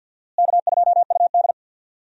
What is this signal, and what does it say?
Morse code, heard as a keyed tone, about 48 wpm, 700 Hz. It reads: K2UD
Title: USA Call Signs - 4 Characters - Repeated 2x - 50wpm